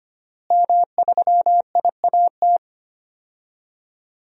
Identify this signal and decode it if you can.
Morse code, heard as a keyed tone, about 25 wpm, 700 Hz. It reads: M3IAT
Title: DXCC-100 Call Signs - 5 Characters - Repeated 2x - 25wpm